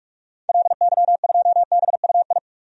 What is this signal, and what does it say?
Morse code, heard as a keyed tone, about 46 wpm, 700 Hz. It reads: LY2BUI
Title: DXCC-100 Call Signs - 6 Characters - 45wpm